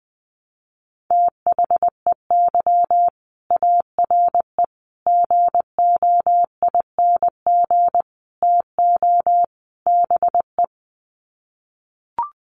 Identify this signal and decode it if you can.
Morse code, heard as a keyed tone, about 20 wpm, 700 Hz. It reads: THEY ARE GOING TO BE
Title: Sets of 5 Words Encore2 20wpm